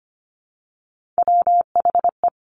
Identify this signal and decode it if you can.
Morse code, heard as a keyed tone, about 25 wpm, 700 Hz. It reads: WHE
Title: Top Three Letter Combinations in Words 25wpm